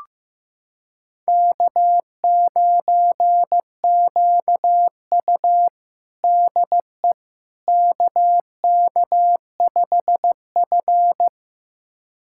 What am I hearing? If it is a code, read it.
Morse code, heard as a keyed tone, about 15 wpm, 700 Hz. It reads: K9QU DE KK5F
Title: USA Callsign DE Callsign - 4 Characters - 15wpm